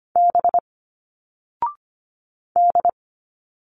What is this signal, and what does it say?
Morse code, heard as a keyed tone, about 25 wpm, 700 Hz. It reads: B D
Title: Single Letters 25wpm